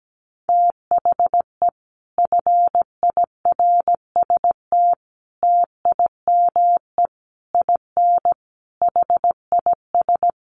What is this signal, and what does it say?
Morse code, heard as a keyed tone, about 17 wpm, 700 Hz. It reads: THE FIRST TIME IN HIS
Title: Sets of 5 Words Encore2 17wpm